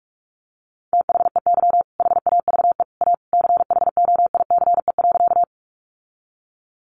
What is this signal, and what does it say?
Morse code, heard as a keyed tone, about 44 wpm, 700 Hz. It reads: THEY HAVE A CHOICE.
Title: Sentences from Top 900 Words 45wpm